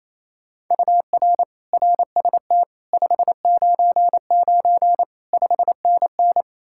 Morse code, heard as a keyed tone, about 35 wpm, 700 Hz. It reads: UR RST 599 5NN
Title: Speed-Racing Common QSO Phrases - 28wpm